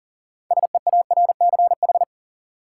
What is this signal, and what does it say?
Morse code, heard as a keyed tone, about 38 wpm, 700 Hz. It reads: SEARCH
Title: Sentences from Top 800 Words - New Word Drill - 40wpm